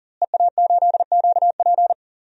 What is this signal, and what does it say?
Morse code, heard as a keyed tone, about 40 wpm, 700 Hz. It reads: EA8QP
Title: DXCC-100 Call Signs - 5 Characters - Repeated 3x - 40wpm